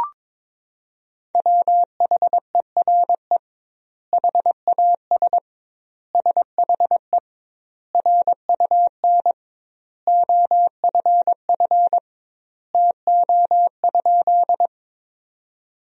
Morse code, heard as a keyed tone, about 22 wpm, 700 Hz. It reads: WHERE HAS SHE RUN OFF TO?
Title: 2x Word-Spacing - Sentences from Top 300 Words - 22wpm